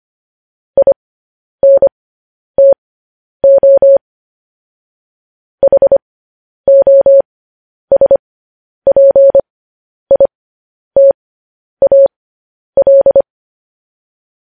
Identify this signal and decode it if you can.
Morse code, heard as a keyed tone, about 25 wpm, spaced as for 10 wpm, 550 Hz. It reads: INTO HOSPITAL
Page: News Headlines in Morse Code at 10 WPM